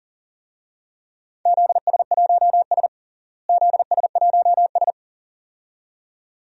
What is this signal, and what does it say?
Morse code, heard as a keyed tone, about 40 wpm, 700 Hz. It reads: ZS1S ZS1S
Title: DXCC-100 Call Signs - 4 Characters - Repeated 2x - 40wpm